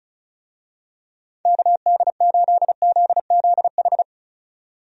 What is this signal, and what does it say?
Morse code, heard as a keyed tone, about 36 wpm, 700 Hz. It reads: KD8ZZH
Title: USA Call Signs - 6 Characters - 35wpm